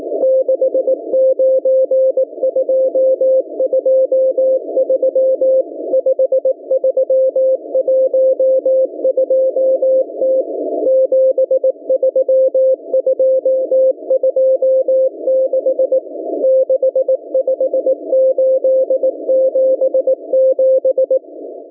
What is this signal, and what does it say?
Morse code, heard as a keyed tone, about 18 wpm, 520 Hz. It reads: 69223 5312T 73226 65877